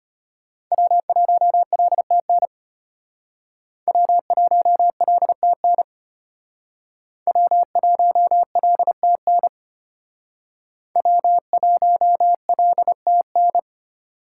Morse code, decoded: W1LTN W1LTN W1LTN W1LTN

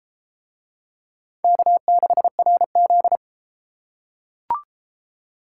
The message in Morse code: K6RZ

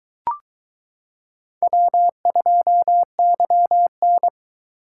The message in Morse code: W2YN

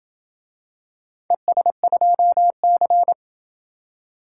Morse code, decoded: ES2C

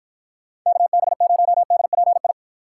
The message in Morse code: KB0DPI

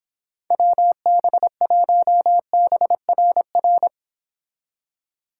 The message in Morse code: WB1BRR

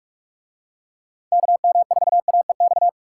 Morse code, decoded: KM4AEX

